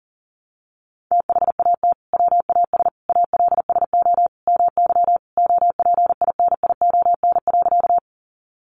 Morse code, heard as a keyed tone, about 40 wpm, 700 Hz. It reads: THAT WAS ALSO MY OPINION.
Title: Sentences from Top 1200 Words 40wpm